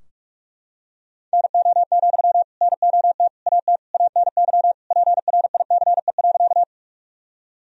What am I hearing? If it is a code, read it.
Morse code, heard as a keyed tone, about 46 wpm, 700 Hz. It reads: NO, NOT AT ANY PRICE.
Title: Sentences from Top 1000 Words - Review Part 4 - 45wpm